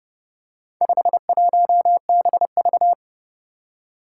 Morse code, heard as a keyed tone, about 29 wpm, 700 Hz. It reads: 51BV
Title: Four Challenging Letter-Number Combinations 30wpm